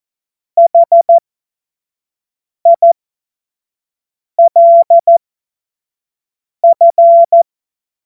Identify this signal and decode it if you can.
Morse code, heard as a keyed tone, about 14 wpm, spaced as for 5 wpm, 680 Hz. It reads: HILF
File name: Morsekurs / Witze / Witz 06. April 2021, 25 F70